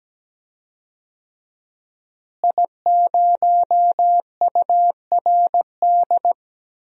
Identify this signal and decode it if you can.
Morse code, heard as a keyed tone, about 17 wpm, 700 Hz. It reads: I0URD
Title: DXCC-100 Call Signs - 5 Characters - Repeated 3x - 17wpm